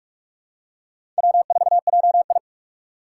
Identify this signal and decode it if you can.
Morse code, heard as a keyed tone, about 46 wpm, 700 Hz. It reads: W4JI
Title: USA Call Signs - 4 Characters - 4x Character Spacing - 45wpm